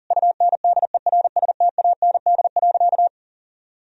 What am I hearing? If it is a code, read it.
Morse code, heard as a keyed tone, about 37 wpm, 700 Hz. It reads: UNDERSTAND.
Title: Sentences from Top 1000 Words - Review Part 3 - 40wpm